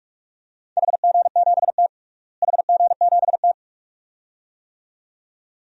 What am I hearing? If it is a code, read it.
Morse code, heard as a keyed tone, about 46 wpm, 700 Hz. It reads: HG7T HG7T
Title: DXCC-100 Call Signs - 4 Characters - Repeated 2x - 45wpm